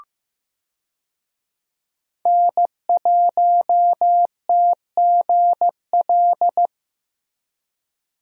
Morse code, heard as a keyed tone, about 15 wpm, 700 Hz. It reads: N1TGL
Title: USA Call Signs - 5 Characters - Repeated 2x - 15wpm